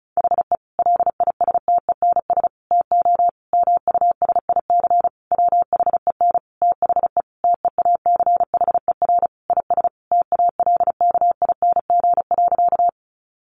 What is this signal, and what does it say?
Morse code, heard as a keyed tone, about 34 wpm, 700 Hz. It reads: HE LISTENS TO MUSIC WHEN THE TEACHER IS TALKING.
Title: Additional Easy Phrases 35wpm